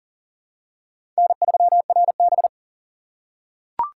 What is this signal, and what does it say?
Morse code, heard as a keyed tone, about 40 wpm, 700 Hz. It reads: N3RB
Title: USA Call Signs - 4 Characters - 4x Character Spacing - 40wpm